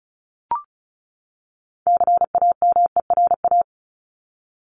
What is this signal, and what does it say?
Morse code, heard as a keyed tone, about 34 wpm, 700 Hz. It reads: CAMERA
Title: Top 501-750 Words 35wpm